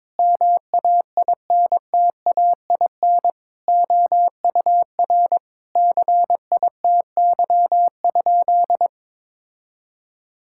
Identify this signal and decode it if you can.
Morse code, heard as a keyed tone, about 22 wpm, 700 Hz. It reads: MAINTAIN OUR CITY?
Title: Sentences from Top 1000 Words - Review Part 5 - 22wpm